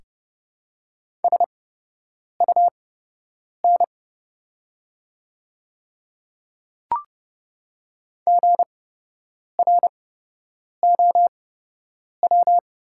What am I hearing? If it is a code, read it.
Morse code, heard as a keyed tone, about 31 wpm, spaced as for 8 wpm, 700 Hz. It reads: SUN GROW
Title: 8x Character Spacing - Top 300 Words - 30wpm